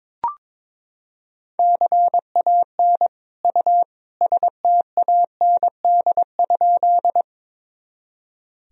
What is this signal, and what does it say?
Morse code, heard as a keyed tone, about 22 wpm, 700 Hz. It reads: CAN U STAND?
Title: Sentences from Top 300 Words and Abbreviations 22wpm